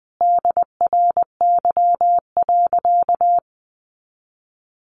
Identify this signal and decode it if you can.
Morse code, heard as a keyed tone, about 20 wpm, 700 Hz. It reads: DRY.
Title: Sentences from Top 500 Words 20wpm